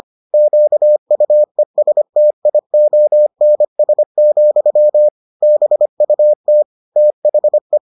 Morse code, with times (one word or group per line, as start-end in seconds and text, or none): 0.3-5.1 QUESTIONS,
5.4-6.6 BUT
7.0-7.8 THE